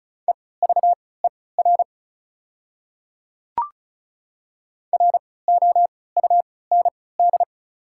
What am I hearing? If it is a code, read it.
Morse code, heard as a keyed tone, about 34 wpm, spaced as for 20 wpm, 690 Hz. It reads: EVER ROUND
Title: 3x Character Spacing - Top 500 Words - 35wpm